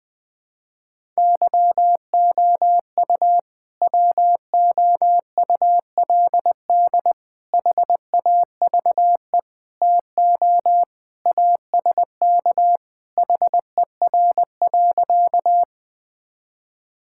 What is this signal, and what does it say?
Morse code, heard as a keyed tone, about 20 wpm, 700 Hz. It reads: YOU WOULD HAVE TO ASK HER.
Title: Sentences with 3-Word Phrases - 20wpm